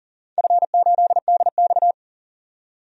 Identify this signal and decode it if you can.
Morse code, heard as a keyed tone, about 42 wpm, 700 Hz. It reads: F8DX
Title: DXCC-100 Call Signs - 4 Characters - 40wpm